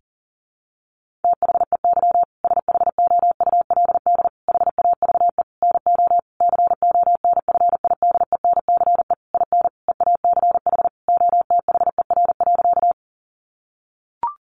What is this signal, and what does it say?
Morse code, heard as a keyed tone, about 40 wpm, 700 Hz. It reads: THEY SHOULD HAVE NO CONFIDENCE IN EACH OTHER.
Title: Sentences from Top 1500 Words 40wpm